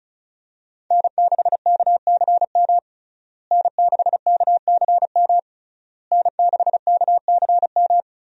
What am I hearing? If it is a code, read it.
Morse code, heard as a keyed tone, about 36 wpm, 700 Hz. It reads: N6KCM N6KCM N6KCM
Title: USA Call Signs - 5 Characters - Repeated 3x - 35wpm